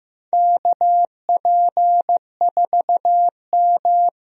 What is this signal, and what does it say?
Morse code, heard as a keyed tone, about 15 wpm, 700 Hz. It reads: KP4M
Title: DXCC-100 Call Signs - 4 Characters - Repeated 2x - 15wpm